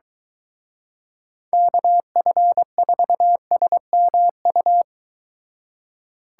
Morse code, heard as a keyed tone, about 23 wpm, 700 Hz. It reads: KF4SMU